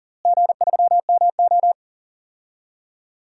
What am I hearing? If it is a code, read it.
Morse code, heard as a keyed tone, about 40 wpm, 700 Hz. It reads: G3MO
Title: DXCC-100 Call Signs - 4 Characters - Repeated 3x - 40wpm